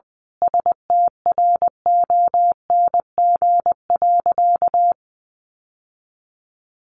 Morse code, heard as a keyed tone, about 20 wpm, 700 Hz. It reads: STRONG.
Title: Sentences from Top 1400 Words 20wpm